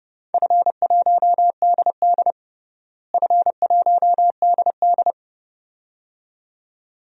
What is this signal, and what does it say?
Morse code, heard as a keyed tone, about 32 wpm, 700 Hz. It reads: F1DD F1DD
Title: DXCC-100 Call Signs - 4 Characters - Repeated 2x - 30wpm